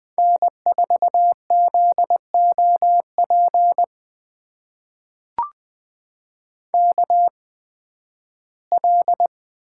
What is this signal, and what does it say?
Morse code, heard as a keyed tone, about 20 wpm, spaced as for 13 wpm, 700 Hz. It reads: N4ZOP KL
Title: USA Call Signs - 8x Character Spacing - 20wpm